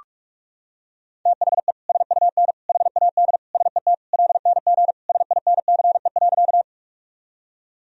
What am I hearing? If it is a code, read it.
Morse code, heard as a keyed tone, about 44 wpm, 700 Hz. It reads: THE SUN HAD SET LNG SINCE.
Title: Sentences from Top 300 Words and Abbreviations 45wpm